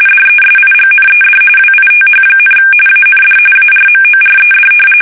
RTTY, about 45.45 baud. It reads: DESIGNED WITH, AND ARMORED
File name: fsk850 / battleships